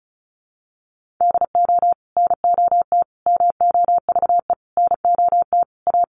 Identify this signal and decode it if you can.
Morse code, heard as a keyed tone, about 34 wpm, 700 Hz. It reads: DO NOT MOVE NOT A
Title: Sentences from Top 1300 Words 35wpm